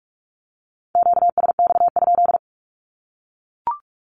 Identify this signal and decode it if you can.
Morse code, heard as a keyed tone, about 44 wpm, 700 Hz. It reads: QSX?
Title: Q-Codes 45wpm